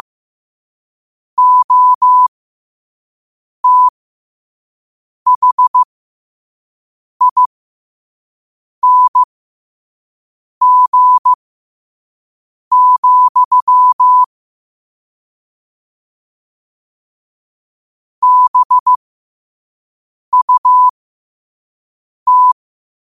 Morse code, heard as a keyed tone, about 15 wpm, spaced as for 5 wpm, 1000 Hz. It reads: OTHING, BUT